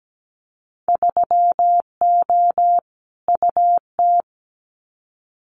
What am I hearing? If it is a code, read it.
Morse code, heard as a keyed tone, about 17 wpm, 700 Hz. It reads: 3O UT
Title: Winter Field Day Exchanges 17wpm